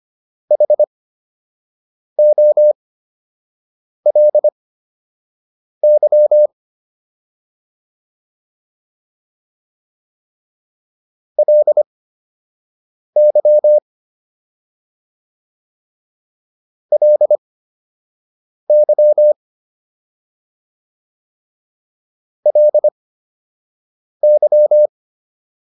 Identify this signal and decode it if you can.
Morse code, heard as a keyed tone, about 25 wpm, spaced as for 6 wpm, 600 Hz. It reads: HOLY LY LY LY